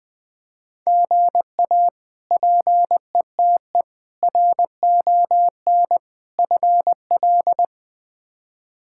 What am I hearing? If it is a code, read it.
Morse code, heard as a keyed tone, about 21 wpm, 700 Hz. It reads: GA PETE RON FL
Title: K1USN SST QSO Practice - 20wpm